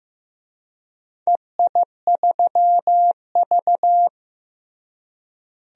Morse code, heard as a keyed tone, about 15 wpm, 700 Hz. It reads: EI3V
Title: DXCC-100 Call Signs - 4 Characters - Repeated 3x - 15wpm